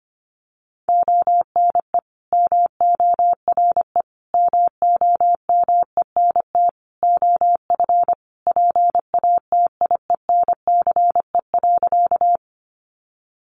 Morse code, heard as a keyed tone, about 25 wpm, 700 Hz. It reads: ONE MORE MOMENT OF PATIENCE.
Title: Sentences from Top 1300 Words 25wpm